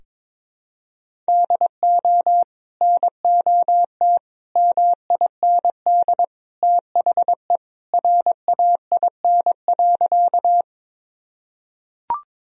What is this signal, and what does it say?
Morse code, heard as a keyed tone, about 22 wpm, 700 Hz. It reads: DO NOT MIND THE RAIN.
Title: Sentences from Top 1000 Words - Review Part 1 - 22wpm